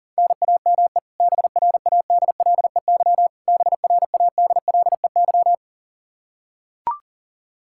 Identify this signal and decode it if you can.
Morse code, heard as a keyed tone, about 40 wpm, 700 Hz. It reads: NAME BRADLEY BRADLEY
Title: Practice QSO - Exchange 1 - Beginning - 40wpm